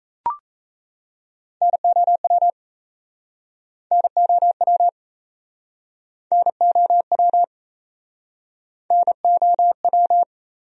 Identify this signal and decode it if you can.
Morse code, heard as a keyed tone, about 39 wpm, 700 Hz. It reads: NOW NOW NOW NOW